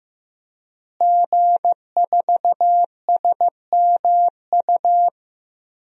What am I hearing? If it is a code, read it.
Morse code, heard as a keyed tone, about 15 wpm, 700 Hz. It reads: G4SMU
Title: DXCC-100 Call Signs - 5 Characters - Repeated 3x - 15wpm